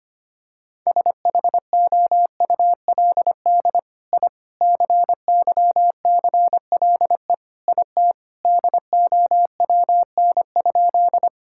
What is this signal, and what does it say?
Morse code, heard as a keyed tone, about 25 wpm, 700 Hz. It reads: SHOULD I CYCLE IT DOWN?
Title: Sentences from Top 1000 Words - Review Part 6 - 25wpm